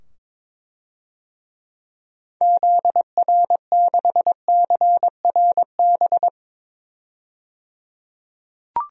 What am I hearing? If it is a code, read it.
Morse code, heard as a keyed tone, about 21 wpm, 700 Hz. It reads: ZR6CRB
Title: DXCC-100 Call Signs - 6 Characters - 22wpm